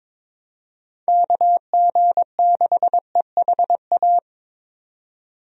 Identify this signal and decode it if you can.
Morse code, heard as a keyed tone, about 22 wpm, 700 Hz. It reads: KG6EHA
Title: USA Call Signs - 6 Characters - 22wpm